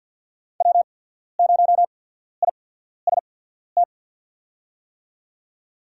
Morse code, heard as a keyed tone, about 52 wpm, spaced as for 14 wpm, 700 Hz. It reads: W0IST
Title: USA Call Signs - 5 Characters - 8x Character Spacing - 50wpm